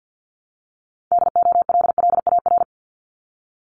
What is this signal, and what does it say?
Morse code, heard as a keyed tone, about 50 wpm, 690 Hz. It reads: DOLLAR
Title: Multi-letter Phonemes Repeated 3x 50wpm